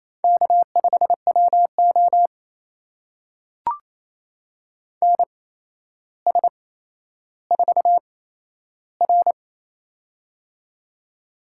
Morse code, heard as a keyed tone, about 27 wpm, spaced as for 12 wpm, 700 Hz. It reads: K5WO NS4R
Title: USA Call Signs - 4 Characters - 8x Character Spacing - 28wpm